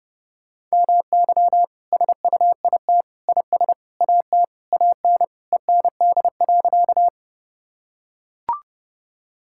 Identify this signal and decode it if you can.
Morse code, heard as a keyed tone, about 30 wpm, 700 Hz. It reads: MY SUIT IS AT AN END.
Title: Sentences from Top 1600 Words 30wpm